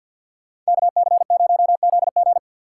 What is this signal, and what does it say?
Morse code, heard as a keyed tone, about 64 wpm, 700 Hz. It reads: KC0ZG